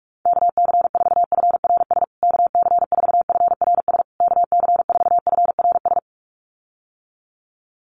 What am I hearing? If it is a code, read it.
Morse code, heard as a keyed tone, about 44 wpm, 700 Hz. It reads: KC4FRS KC4FRS KC4FRS
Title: USA Call Signs Repeated 3x 45wpm